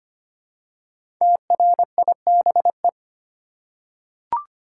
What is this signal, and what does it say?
Morse code, heard as a keyed tone, about 25 wpm, 700 Hz. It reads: TRIBE
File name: 5 Letter Words - First 3 Letters Given - 25wpm